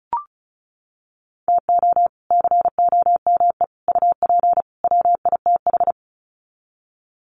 Morse code, heard as a keyed tone, about 34 wpm, 700 Hz. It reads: TO COME UP WITH
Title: Sets of 4 Words 35wpm